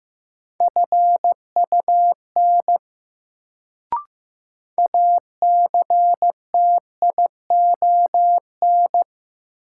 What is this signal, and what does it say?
Morse code, heard as a keyed tone, about 15 wpm, 700 Hz. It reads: FUN ACTION
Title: Sentences from Top 700 Words - New Word Drill - 15wpm